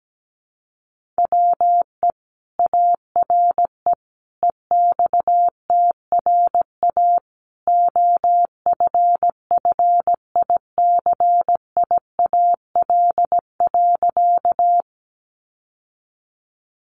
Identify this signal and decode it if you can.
Morse code, heard as a keyed tone, about 17 wpm, 700 Hz. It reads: WE ARE EXTRA OFFICIAL.
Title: Sentences from Top 1000 Words 17wpm